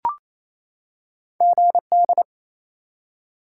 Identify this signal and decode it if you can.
Morse code, heard as a keyed tone, about 28 wpm, 700 Hz. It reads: GD
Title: Top 90 QSO Elements 28wpm